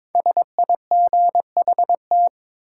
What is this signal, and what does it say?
Morse code, heard as a keyed tone, about 23 wpm, 700 Hz. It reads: SIGHT
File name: 5 Letter Words - First 3 Letters Given - 22wpm